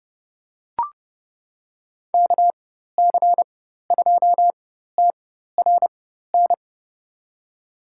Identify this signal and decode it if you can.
Morse code, heard as a keyed tone, about 30 wpm, spaced as for 14 wpm, 700 Hz. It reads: KC2TRN
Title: USA Call Signs - 6 Characters - 4x Character Spacing - 30wpm